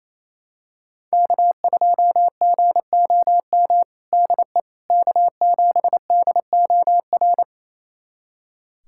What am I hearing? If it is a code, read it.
Morse code, heard as a keyed tone, about 28 wpm, 700 Hz. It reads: K2GOM DE K7DOR
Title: USA Callsign DE Callsign - 5 Characters - 28wpm